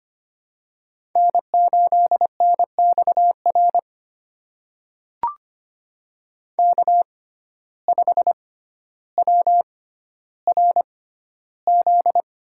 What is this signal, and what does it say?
Morse code, heard as a keyed tone, about 25 wpm, spaced as for 13 wpm, 700 Hz. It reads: N8NXR K5WRZ
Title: USA Call Signs - 5 Characters - 6x Character Spacing - 25wpm